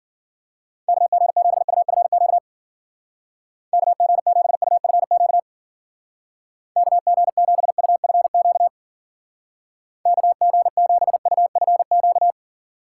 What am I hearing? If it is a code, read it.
Morse code, heard as a keyed tone, about 51 wpm, 700 Hz. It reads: KG7UFQ KG7UFQ KG7UFQ KG7UFQ